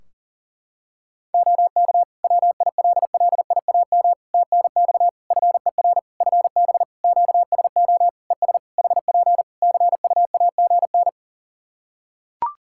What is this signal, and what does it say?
Morse code, heard as a keyed tone, about 40 wpm, 700 Hz. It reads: OK WILLIAM TNX FER FB QSO ES HP CUAGN
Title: Practice QSO - Exchange 4 - Ending - 40wpm